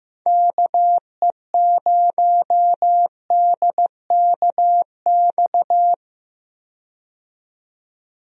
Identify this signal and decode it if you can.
Morse code, heard as a keyed tone, about 15 wpm, 700 Hz. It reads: KE0DKX
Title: USA Call Signs - 6 Characters - 15wpm